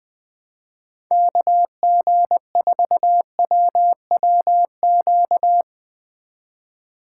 Morse code, heard as a keyed tone, about 20 wpm, 700 Hz. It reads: KG4WWQ
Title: USA Call Signs - 6 Characters - 4x Character Spacing - 20wpm